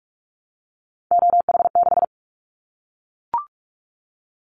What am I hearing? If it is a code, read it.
Morse code, heard as a keyed tone, about 44 wpm, 700 Hz. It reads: OH6
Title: Callsign Prefixes for Top 25 DXCC - 45wpm